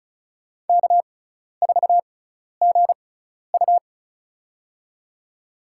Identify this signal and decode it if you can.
Morse code, heard as a keyed tone, about 34 wpm, spaced as for 12 wpm, 700 Hz. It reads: K4GU
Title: USA Call Signs - 6x Character Spacing - 35wpm